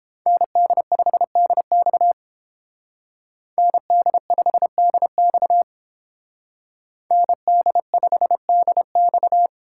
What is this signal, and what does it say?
Morse code, heard as a keyed tone, about 29 wpm, 700 Hz. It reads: ND5DX ND5DX ND5DX